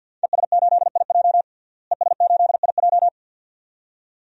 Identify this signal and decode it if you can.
Morse code, heard as a keyed tone, about 52 wpm, 700 Hz. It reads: ES8IJ ES8IJ
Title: DXCC-100 Call Signs - 5 Characters - Repeated 2x - 50wpm